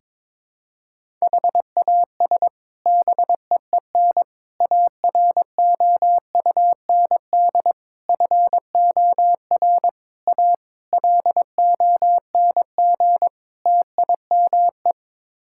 Morse code, heard as a keyed tone, about 22 wpm, 700 Hz. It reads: HAS BEEN AROUND FOR A LONG TIME
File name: practice / 22wpm / Sets of 7 Words - 22wpm